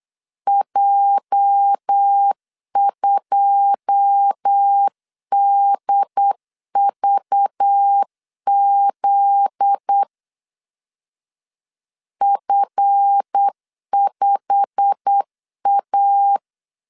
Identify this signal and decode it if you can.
Morse code, heard as a keyed tone, about 8 wpm, 800 Hz. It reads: J2DVZ F5A
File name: CW / DET / 1 9 1500C S C 8wpm 5653